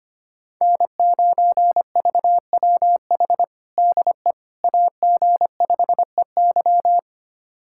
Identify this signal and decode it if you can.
Morse code, heard as a keyed tone, about 25 wpm, 700 Hz. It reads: N9VWH DE AG5EY
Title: USA Callsign DE Callsign - 5 Characters - 25wpm